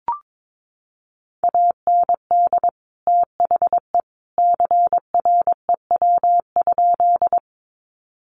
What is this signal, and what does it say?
Morse code, heard as a keyed tone, about 22 wpm, 700 Hz. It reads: AND THE CREW?
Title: Sentences from Top 1500 Words 22wpm